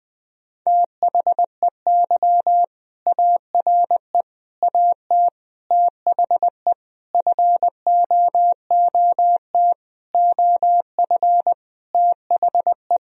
Morse code, heard as a keyed tone, about 20 wpm, 700 Hz. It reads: THEY ARE AT THE FOOT OF THE
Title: Sentences from Top 1000 Words - Review Part 4 - 20wpm